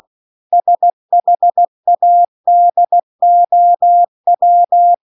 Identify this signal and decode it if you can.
Morse code, heard as a keyed tone, about 16 wpm, 700 Hz. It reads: SHADOW
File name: CW-The-Big-Twoheaded-River-1-16WPM